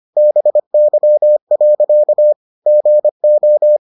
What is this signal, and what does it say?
Morse code, heard as a keyed tone, about 25 wpm, 600 Hz. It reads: BY. GO